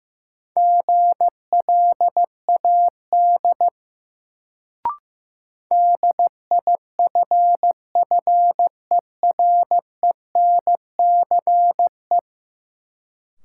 Morse code, heard as a keyed tone, about 15 wpm, 700 Hz. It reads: GLAD DIFFERENCE
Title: Sentences from Top 800 Words - New Word Drill - 15wpm